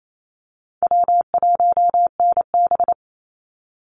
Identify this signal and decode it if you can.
Morse code, heard as a keyed tone, about 28 wpm, 700 Hz. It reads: W1NB
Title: USA Call Signs 28wpm